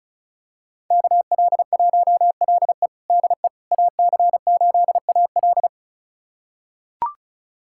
Morse code, heard as a keyed tone, about 36 wpm, 700 Hz. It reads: KL1LE DE AC8AL
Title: USA Callsign DE Callsign - 5 Characters - 35wpm